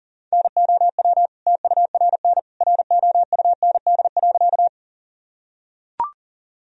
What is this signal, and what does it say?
Morse code, heard as a keyed tone, about 40 wpm, 700 Hz. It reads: NOW TURN ROUND.
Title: Sentences from Top 1000 Words - Review Part 1 - 40wpm